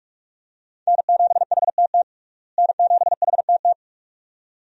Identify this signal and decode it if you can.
Morse code, heard as a keyed tone, about 46 wpm, 700 Hz. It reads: N7HTT N7HTT
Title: USA Call Signs - 5 Characters - Repeated 2x - 45wpm